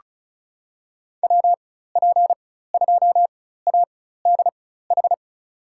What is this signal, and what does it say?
Morse code, heard as a keyed tone, about 36 wpm, spaced as for 16 wpm, 700 Hz. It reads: WP2ADH
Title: USA Call Signs - 4x Character Spacing - 35wpm